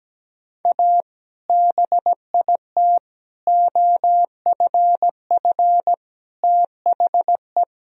Morse code, heard as a keyed tone, about 17 wpm, 700 Hz. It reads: A BIT OFF THE
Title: Sentences from Top 1000 Words - Review Part 2 - 17wpm